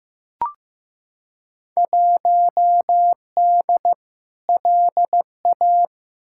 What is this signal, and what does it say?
Morse code, heard as a keyed tone, about 15 wpm, 700 Hz. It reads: 1D LA
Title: Field Day Exchanges - 15wpm